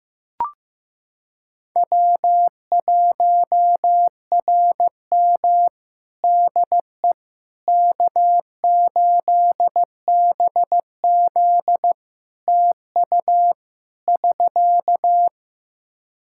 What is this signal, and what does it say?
Morse code, heard as a keyed tone, about 15 wpm, 700 Hz. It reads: W1RM DE K8BZ TU <SK>
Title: Practice QSO - Exchange 4 - Ending - 15wpm